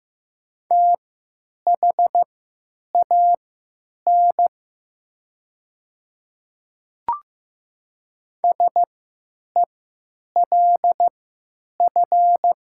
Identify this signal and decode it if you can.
Morse code, heard as a keyed tone, about 15 wpm, spaced as for 9 wpm, 700 Hz. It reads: THAN SELF
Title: 3x Character Spacing - Top 200 Words - 15wpm